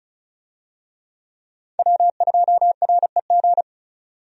Morse code, heard as a keyed tone, about 36 wpm, 700 Hz. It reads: W2REG
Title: USA Call Signs - 5 Characters - 35wpm